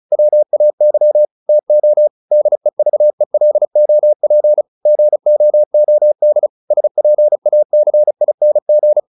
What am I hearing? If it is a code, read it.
Morse code, heard as a keyed tone, about 34 wpm, 600 Hz. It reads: WAY TO DEVELOP GOOD SPACING